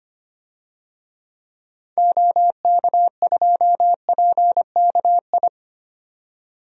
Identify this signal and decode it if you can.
Morse code, heard as a keyed tone, about 25 wpm, 700 Hz. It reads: OK2PKI
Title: DXCC-100 Call Signs - 6 Characters - 25wpm